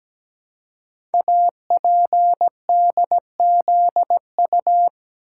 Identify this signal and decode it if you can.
Morse code, heard as a keyed tone, about 17 wpm, 700 Hz. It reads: APDZU
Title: Code Groups - 5 Letters - 17wpm